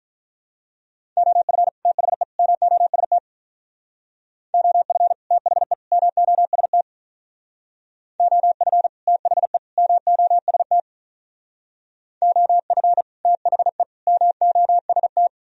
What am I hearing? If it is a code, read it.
Morse code, heard as a keyed tone, about 44 wpm, 700 Hz. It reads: OF THE MOST OF THE MOST OF THE MOST OF THE MOST